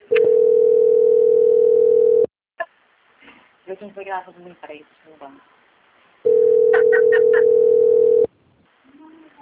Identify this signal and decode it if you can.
Ringback tone (North America), 440 + 480 Hz, with speech.
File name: out- - - - - .74992